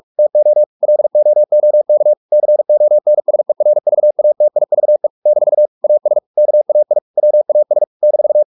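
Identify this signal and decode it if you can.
Morse code, heard as a keyed tone, about 44 wpm, 600 Hz. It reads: TO LOOK CONSERVATIVE - AS KAI WAS -
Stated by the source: Morse practice at 45 wpm, the novel Never by Ken Follett in English, from CW med Gnister.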